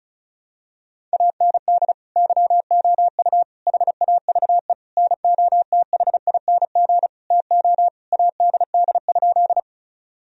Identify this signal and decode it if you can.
Morse code, heard as a keyed tone, about 36 wpm, 700 Hz. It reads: AND YOU HAVE NOTHING TO ADD?
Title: Sentences from Top 1000 Words - Review Part 1 - 35wpm